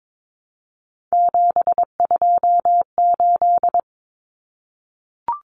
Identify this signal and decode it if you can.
Morse code, heard as a keyed tone, about 22 wpm, 700 Hz. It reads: 728
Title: Common Cut Numbers - 3-Digit 22wpm